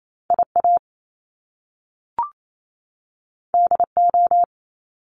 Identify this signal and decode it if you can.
Morse code, heard as a keyed tone, about 28 wpm, 700 Hz. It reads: IA DO
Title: Top Two Letter Combinations in Words 28wpm